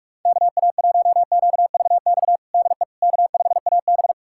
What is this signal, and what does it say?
Morse code, heard as a keyed tone, about 46 wpm, 700 Hz. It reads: KA1QVX DE K5AB
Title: USA Callsign DE Callsign 45wpm